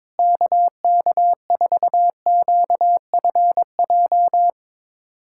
Morse code, heard as a keyed tone, about 22 wpm, 700 Hz. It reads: KK4QFJ
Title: USA Callsign DE Callsign - 6 Characters - 22wpm